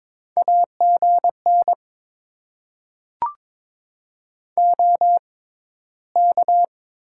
Morse code, heard as a keyed tone, about 22 wpm, spaced as for 14 wpm, 700 Hz. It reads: AGN OK
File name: Top 90 QSO Elements - 6x Character Spacing - 22wpm